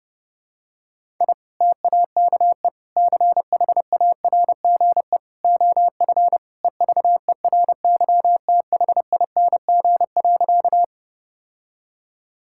Morse code, heard as a keyed tone, about 30 wpm, 700 Hz. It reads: I TAKE CHARGE OF EVERYTHING.
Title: Sentences from Top 1000 Words - Review Part 6 - 30wpm